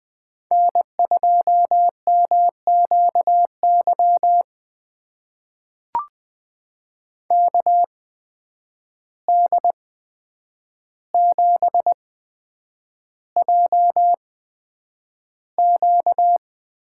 Morse code, heard as a keyed tone, about 20 wpm, spaced as for 9 wpm, 700 Hz. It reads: N2MQY KD7JQ
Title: USA Call Signs - 5 Characters - 8x Character Spacing - 20wpm